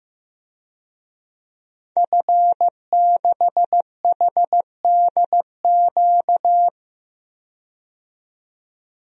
Morse code, heard as a keyed tone, about 16 wpm, 700 Hz. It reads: F6HDQ
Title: DXCC-100 Call Signs - 5 Characters - 15wpm